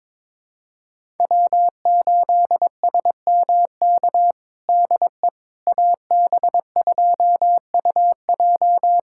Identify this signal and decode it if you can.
Morse code, heard as a keyed tone, about 21 wpm, 700 Hz. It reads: W8SMK DE AB2UJ
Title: USA Callsign DE Callsign - 5 Characters - 22wpm